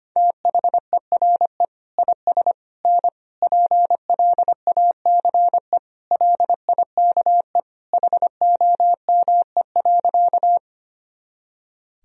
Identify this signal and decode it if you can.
Morse code, heard as a keyed tone, about 25 wpm, 700 Hz. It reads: THERE IS N PLACE LIKE HOME.
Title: Sentences from Top 200 Words and Abbreviations 25wpm